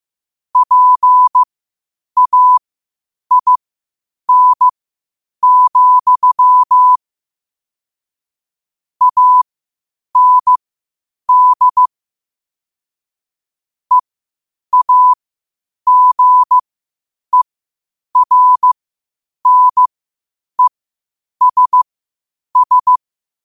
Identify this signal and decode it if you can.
Morse code, heard as a keyed tone, about 15 wpm, spaced as for 8 wpm, 1000 Hz. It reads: PAIN, AND EAGERNESS
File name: qotdfiles / 8wpm